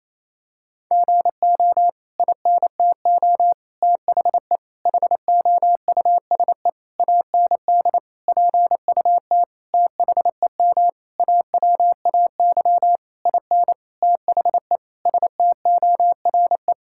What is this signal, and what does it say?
Morse code, heard as a keyed tone, about 28 wpm, 700 Hz. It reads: GO INTO THE HOUSE AND PUT THEM AWAY IN THE STORE
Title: Sentences from Top 1000 Words - Review Part 5 - 28wpm